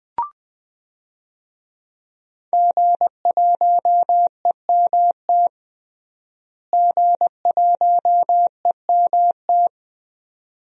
Morse code, heard as a keyed tone, about 20 wpm, 700 Hz. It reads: G1EMT G1EMT
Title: DXCC-100 Call Signs - 5 Characters - Repeated 2x - 20wpm